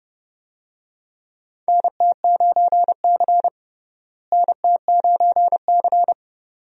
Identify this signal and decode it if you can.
Morse code, heard as a keyed tone, about 32 wpm, 700 Hz. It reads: NT9C NT9C
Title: USA Call Signs - 4 Characters - Repeated 2x - 30wpm